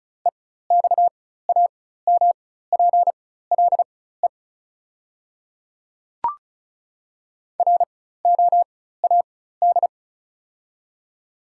Morse code, heard as a keyed tone, about 36 wpm, spaced as for 16 wpm, 700 Hz. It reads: EXAMPLE ROAD
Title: 4x Character Spacing - Top 500 Words - 35wpm